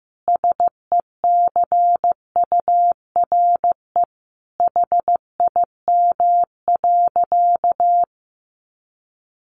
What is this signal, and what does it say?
Morse code, heard as a keyed tone, about 15 wpm, 700 Hz. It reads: SECURE HIM.
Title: Sentences from Top 1200 Words 15wpm